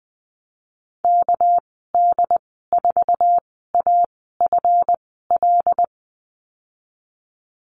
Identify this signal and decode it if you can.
Morse code, heard as a keyed tone, about 20 wpm, spaced as for 15 wpm, 700 Hz. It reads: KD4AFL
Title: 207 Lesson 33g - Letter j - Call Signs 20wpm